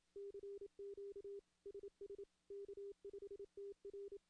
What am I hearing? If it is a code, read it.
Morse code, heard as a keyed tone, about 26 wpm, 400 Hz. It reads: CQ SS K5TR
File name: CW / CW-SS-k5tr- -1700